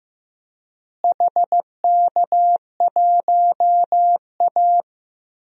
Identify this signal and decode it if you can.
Morse code, heard as a keyed tone, about 15 wpm, 700 Hz. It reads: HK1A
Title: DXCC-100 Call Signs - 4 Characters - Repeated 2x - 15wpm